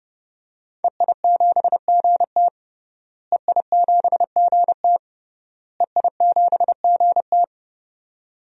Morse code, heard as a keyed tone, about 32 wpm, 700 Hz. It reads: EI7GT EI7GT EI7GT
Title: DXCC-100 Call Signs - 5 Characters - Repeated 3x - 30wpm